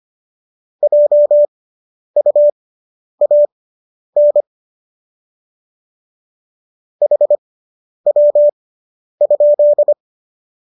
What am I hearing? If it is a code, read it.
Morse code, heard as a keyed tone, about 25 wpm, spaced as for 10 wpm, 590 Hz. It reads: JUAN HW?